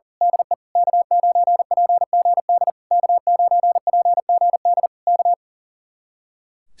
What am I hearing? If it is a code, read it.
Morse code, heard as a keyed tone, about 38 wpm, 700 Hz. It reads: DE K9PGD K9PGD K
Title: Speed-Racing Common QSO Phrases Encore - 40wpm